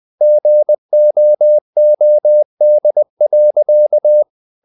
Morse code, heard as a keyed tone, about 20 wpm, 600 Hz. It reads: GOOD.
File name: the old man and the sea part 2 20wpm